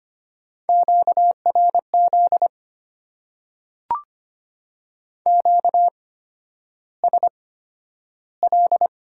Morse code, heard as a keyed tone, about 25 wpm, spaced as for 11 wpm, 690 Hz. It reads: QRZ QSL
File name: Top 90 QSO Elements - 8x Character Spacing - 25wpm